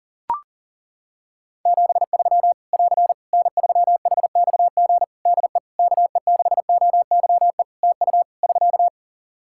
Morse code, heard as a keyed tone, about 40 wpm, 700 Hz. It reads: 73 <AR> N3HXG DE KE6OYE TU <SK>
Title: Practice QSO - Exchange 4 - Ending - 40wpm